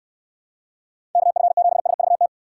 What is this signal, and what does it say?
Morse code, heard as a keyed tone, about 69 wpm, 700 Hz. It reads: KF7SVT